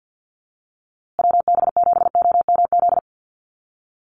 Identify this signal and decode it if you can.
Morse code, heard as a keyed tone, about 50 wpm, 700 Hz. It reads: WB7OMZ
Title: USA Call Signs Repeated 2x 50wpm